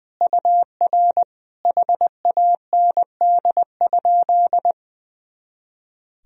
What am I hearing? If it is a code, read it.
Morse code, heard as a keyed tone, about 20 wpm, 700 Hz. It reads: UR HAND?
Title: Sentences from Top 200 Words and Abbreviations 20wpm